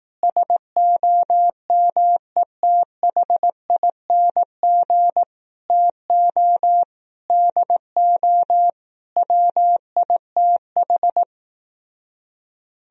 Morse code, heard as a keyed tone, about 18 wpm, 700 Hz. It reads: SOMETHING TO DO WITH
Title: Speed-Racing - Sets of 5 Words 15wpm